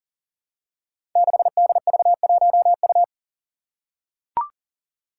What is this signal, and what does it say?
Morse code, heard as a keyed tone, about 40 wpm, 700 Hz. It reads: 6DV1U
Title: Five Challenging Letter-Number Combinations 40wpm